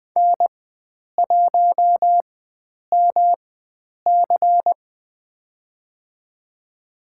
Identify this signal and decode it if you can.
Morse code, heard as a keyed tone, about 20 wpm, spaced as for 9 wpm, 700 Hz. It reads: N1MC
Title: USA Call Signs - 4x Character Spacing - 20wpm